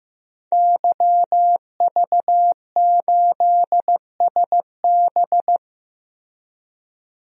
Morse code, heard as a keyed tone, about 15 wpm, 700 Hz. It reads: YV8SB
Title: DXCC-100 Call Signs - 5 Characters - 15wpm